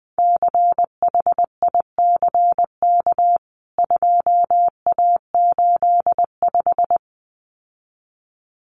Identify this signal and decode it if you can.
Morse code, heard as a keyed tone, about 20 wpm, 700 Hz. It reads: CHICK 2A85
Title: CWT Exchange 20wpm